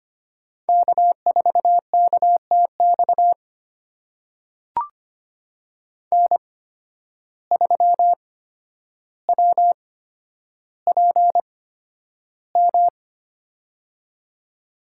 Morse code, heard as a keyed tone, about 25 wpm, spaced as for 11 wpm, 700 Hz. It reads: K4KTX N3WPM
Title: USA Call Signs - 5 Characters - 8x Character Spacing - 25wpm